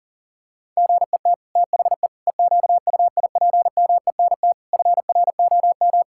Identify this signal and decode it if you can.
Morse code, heard as a keyed tone, about 40 wpm, 700 Hz. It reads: GET THE EQUIPMENT FROM
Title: Sentences from Top 1000 Words - Review Part 5 - 40wpm